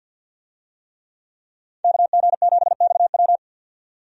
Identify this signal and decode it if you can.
Morse code, heard as a keyed tone, about 53 wpm, 700 Hz. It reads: KG7XW
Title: USA Call Signs - 5 Characters - 50wpm